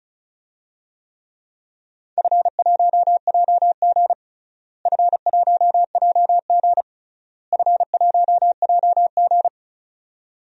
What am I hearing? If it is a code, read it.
Morse code, heard as a keyed tone, about 36 wpm, 700 Hz. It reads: F1JG F1JG F1JG
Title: DXCC-100 Call Signs - 4 Characters - Repeated 3x - 35wpm